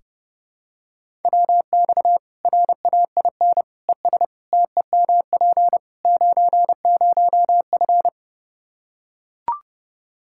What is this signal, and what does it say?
Morse code, heard as a keyed tone, about 32 wpm, 700 Hz. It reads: WX RAIN ES TEMP 90F
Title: Practice QSO - Exchange 2 - Info - 30wpm